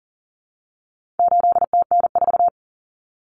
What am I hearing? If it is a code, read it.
Morse code, heard as a keyed tone, about 40 wpm, 700 Hz. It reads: 8TN4
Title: Common Cut Numbers - 4-Digit 40wpm